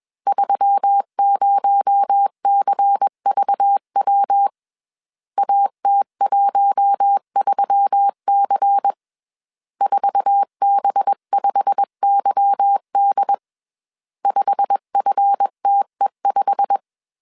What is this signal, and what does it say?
Morse code, heard as a keyed tone, about 21 wpm, 800 Hz. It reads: 30CVW AT13C 4B5YD 5FTE5
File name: CW / DET / 1 9 1500C S C 20wpm 2310